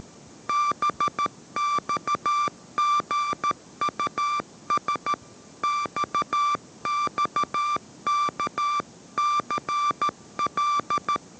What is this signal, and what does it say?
Morse code, heard as a keyed tone, about 13 wpm, 1180 Hz. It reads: BXGUS XXKCL
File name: 14wpm